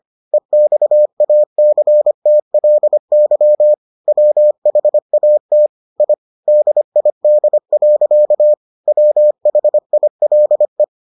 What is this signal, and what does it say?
Morse code, heard as a keyed tone, about 25 wpm, 600 Hz. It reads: EXACTLY WHAT I DID. WHILE